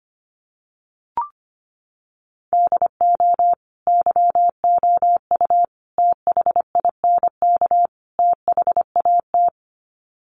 Morse code, heard as a keyed tone, about 25 wpm, 700 Hz. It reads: DO YOU THINK THAT
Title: Sets of 4 Words 25wpm